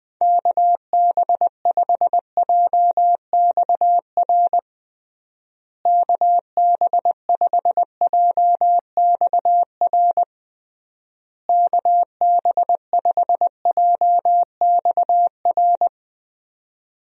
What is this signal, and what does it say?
Morse code, heard as a keyed tone, about 20 wpm, 700 Hz. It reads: KB5JXR KB5JXR KB5JXR
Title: USA Call Signs - 6 Characters - Repeated 3x - 20wpm